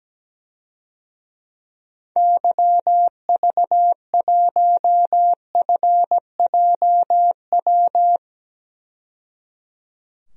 Morse code, heard as a keyed tone, about 17 wpm, 700 Hz. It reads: YV1FJW
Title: DXCC-100 Call Signs - 6 Characters - 17wpm